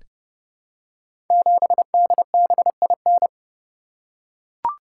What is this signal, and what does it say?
Morse code, heard as a keyed tone, about 30 wpm, 700 Hz. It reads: 7DBIN
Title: Code Groups - 5 Letter:Numbers - 30wpm